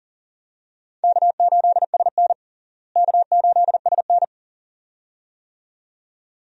Morse code, heard as a keyed tone, about 37 wpm, 700 Hz. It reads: K8SN K8SN
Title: USA Call Signs - 4 Characters - Repeated 2x - 40wpm